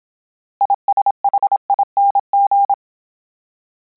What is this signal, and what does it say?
Morse code, heard as a keyed tone, about 27 wpm, 800 Hz. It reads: ISHING